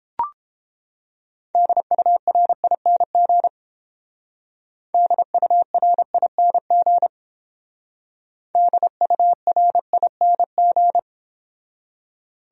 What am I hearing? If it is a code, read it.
Morse code, heard as a keyed tone, about 28 wpm, 700 Hz. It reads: DURING DURING DURING